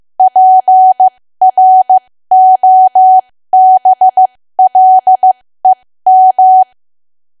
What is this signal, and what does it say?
Morse code, heard as a keyed tone, about 15 wpm, spaced as for 13 wpm, 750 Hz. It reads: PROBLEM
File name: cwa / W1AW-Level2 / 13WPM